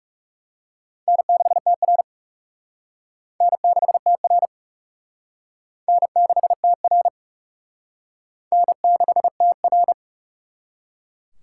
Morse code, heard as a keyed tone, about 39 wpm, 700 Hz. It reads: N6TR N6TR N6TR N6TR